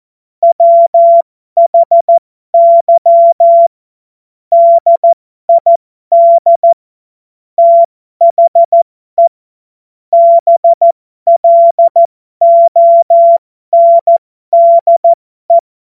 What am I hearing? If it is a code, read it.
Morse code, heard as a keyed tone, about 14 wpm, spaced as for 12 wpm, 680 Hz. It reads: WHY DID THE BLONDE
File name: Morsekurs / Witze